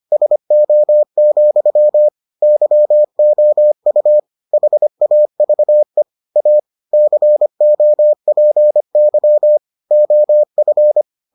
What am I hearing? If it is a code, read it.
Morse code, heard as a keyed tone, about 25 wpm, 600 Hz. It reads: SO, YOU HAVE A COPY OF